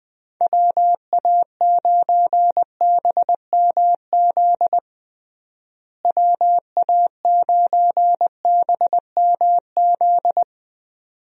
Morse code, heard as a keyed tone, about 20 wpm, 700 Hz. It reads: WA9BMZ WA9BMZ
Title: USA Call Signs - 6 Characters - Repeated 2x - 20wpm